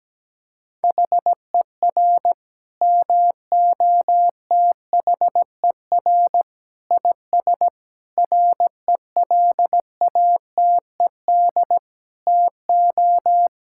Morse code, heard as a keyed tone, about 17 wpm, 700 Hz. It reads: HER MOTHER IS RELATED TO
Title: Sentences from Top 1000 Words - Review Part 5 - 17wpm